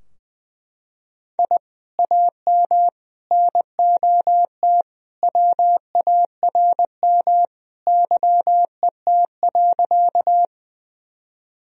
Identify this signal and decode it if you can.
Morse code, heard as a keyed tone, about 21 wpm, 700 Hz. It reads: I AM NOT WARM YET.
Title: Sentences from Top 1000 Words - Review Part 2 - 20wpm